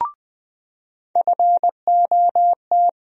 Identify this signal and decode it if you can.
Morse code, heard as a keyed tone, about 20 wpm, 700 Hz. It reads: FOT
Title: USA Call Sign Postfixes - 3 Letters 20wpm